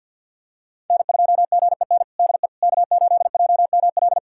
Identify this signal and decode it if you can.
Morse code, heard as a keyed tone, about 50 wpm, 700 Hz. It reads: N2GEN DE K8JML
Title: USA Callsign DE Callsign - 5 Characters - 50wpm